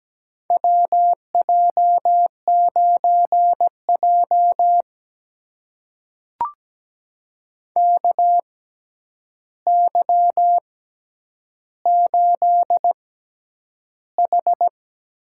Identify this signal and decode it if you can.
Morse code, heard as a keyed tone, about 17 wpm, spaced as for 9 wpm, 700 Hz. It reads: WJ9J KY8H
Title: USA Call Signs - 4 Characters - 6x Character Spacing - 17wpm